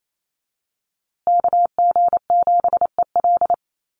Morse code, heard as a keyed tone, about 28 wpm, 700 Hz. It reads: KG7EL
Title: USA Call Signs Repeated 3x 28wpm